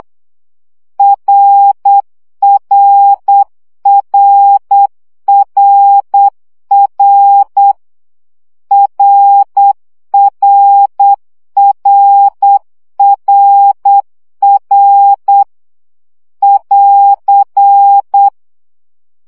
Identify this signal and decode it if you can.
Morse code, heard as a keyed tone, about 8 wpm, 800 Hz. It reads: RRRRR RRRRR <AR>
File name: CW gyakorlas r 7 baud